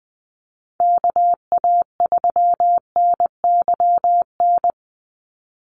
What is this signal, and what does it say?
Morse code, heard as a keyed tone, about 20 wpm, 700 Hz. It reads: KA3NYN
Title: USA Call Signs 20wpm